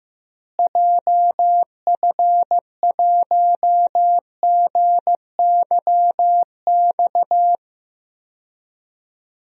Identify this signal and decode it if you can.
Morse code, heard as a keyed tone, about 15 wpm, 700 Hz. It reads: JF1GYX
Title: DXCC-100 Call Signs - 6 Characters - Repeated 3x - 15wpm